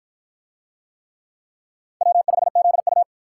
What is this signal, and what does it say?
Morse code, heard as a keyed tone, about 52 wpm, 700 Hz. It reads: W5ZU